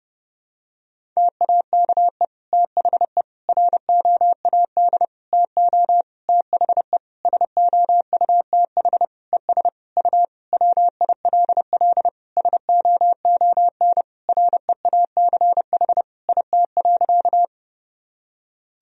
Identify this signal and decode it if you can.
Morse code, heard as a keyed tone, about 30 wpm, 700 Hz. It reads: TAKE THE ROAD TO THE SOUTH ES U WILL SOON REACH IT.
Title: Sentences from Top 400 Words and Abbreviations 30wpm